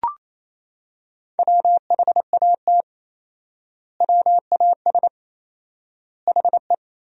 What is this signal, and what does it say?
Morse code, heard as a keyed tone, about 28 wpm, 700 Hz. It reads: WHAT WAS HE
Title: 4x Word-Spacing - Sentences from Top 100 Words - 28wpm